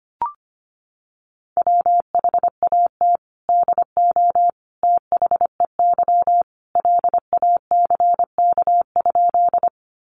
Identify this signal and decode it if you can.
Morse code, heard as a keyed tone, about 25 wpm, 700 Hz. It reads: WHAT DO THEY LACK?
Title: Sentences from Top 1100 Words 25wpm